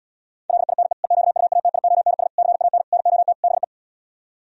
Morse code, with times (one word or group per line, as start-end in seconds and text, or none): 0.5-0.9 LINE
1.0-2.3 EQUATIONS
2.4-2.8 CAN
2.9-3.3 NOT
3.4-3.6 BE